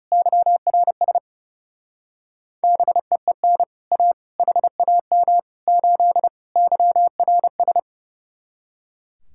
Morse code, heard as a keyed tone, about 39 wpm, 700 Hz. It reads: YRS BEEN A HAM 8 YRS